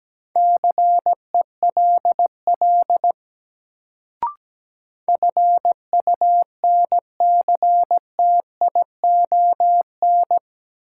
Morse code, heard as a keyed tone, about 17 wpm, 700 Hz. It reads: CELL FUNCTION
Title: Sentences from Top 1100 Words - New Word Drill Alt - 17wpm